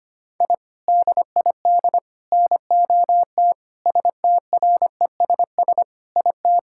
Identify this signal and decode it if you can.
Morse code, heard as a keyed tone, about 25 wpm, 700 Hz. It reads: I DID NOT STRESS IT
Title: Sentences from Top 1000 Words - Review Part 6 - 25wpm